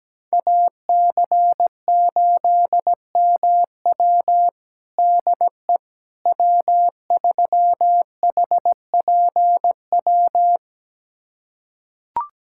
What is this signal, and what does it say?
Morse code, heard as a keyed tone, about 17 wpm, 700 Hz. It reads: AC8MW DE W3HPW
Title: USA Callsign DE Callsign - 5 Characters - 17wpm